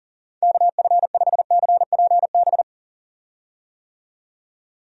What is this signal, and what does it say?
Morse code, heard as a keyed tone, about 40 wpm, 700 Hz. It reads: KF5CPB
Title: USA Call Signs - 6 Characters - 40wpm